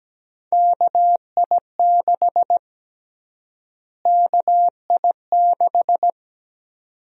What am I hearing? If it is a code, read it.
Morse code, heard as a keyed tone, about 17 wpm, 700 Hz. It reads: KI6 KI6
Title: USA Call Sign Prefixes - Repeated 2x - 17wpm